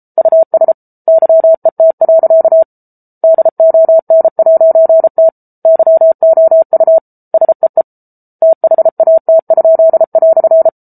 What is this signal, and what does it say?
Morse code, heard as a keyed tone, about 34 wpm, 650 Hz. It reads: US YET. DON'T YOU SEE THAT?"